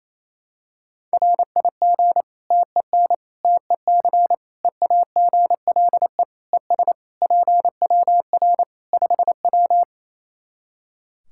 Morse code, heard as a keyed tone, about 27 wpm, 700 Hz. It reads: RIG TEN TEC EAGLE ES PWR 5W
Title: Practice QSO - Exchange 2 - Info - 28wpm